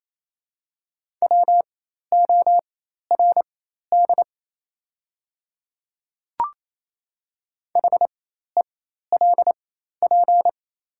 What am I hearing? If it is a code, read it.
Morse code, heard as a keyed tone, about 28 wpm, spaced as for 13 wpm, 700 Hz. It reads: WORD HELP
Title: 4x Character Spacing - Top 200 Words - 28wpm